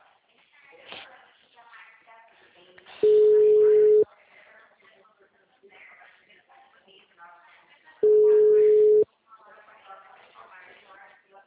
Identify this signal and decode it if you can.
Ringback tone (Europe, 425 Hz), with speech.